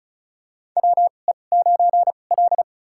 Morse code, heard as a keyed tone, about 34 wpm, spaced as for 25 wpm, 700 Hz. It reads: WE9L
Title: USA Call Signs - 4 Characters - 2x Character Spacing - 35wpm